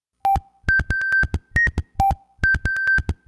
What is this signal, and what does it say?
There is no code.